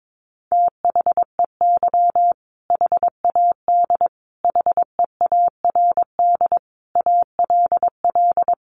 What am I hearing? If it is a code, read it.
Morse code, heard as a keyed tone, about 22 wpm, 700 Hz. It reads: THEY HAD HEARD ALL
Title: Sentences from Top 400 Words 22wpm